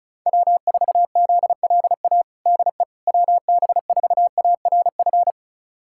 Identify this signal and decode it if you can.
Morse code, heard as a keyed tone, about 36 wpm, 700 Hz. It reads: W4ZLA DE WB4ARF
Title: USA Callsign DE Callsign 35wpm